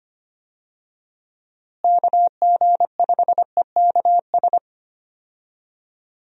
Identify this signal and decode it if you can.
Morse code, heard as a keyed tone, about 25 wpm, 700 Hz. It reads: KG5EKS
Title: USA Call Signs - 6 Characters - Repeated 2x - 25wpm